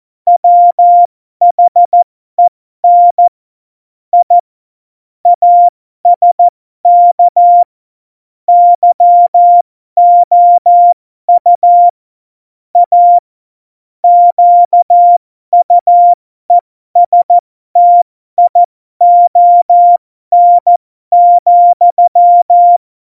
Morse code, heard as a keyed tone, about 14 wpm, spaced as for 12 wpm, 700 Hz. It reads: WHEN I ASK YOU A QUESTION,
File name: Morsekurs / Witze